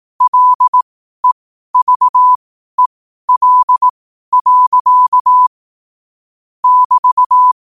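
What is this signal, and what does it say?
Morse code, heard as a keyed tone, about 18 wpm, spaced as for 12 wpm, 1000 Hz. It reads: LEVEL. <BT>
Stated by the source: 12 wpm Morse code.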